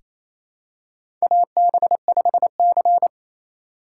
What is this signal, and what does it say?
Morse code, heard as a keyed tone, about 28 wpm, 700 Hz. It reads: AB5C
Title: K1USN SST QSO Practice - 28wpm